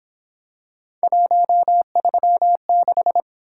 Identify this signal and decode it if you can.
Morse code, heard as a keyed tone, about 28 wpm, 700 Hz. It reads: 136